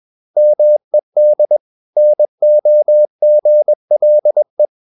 Morse code, heard as a keyed tone, about 21 wpm, 600 Hz. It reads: MED NOGLE
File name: Ferie QRS 21wpm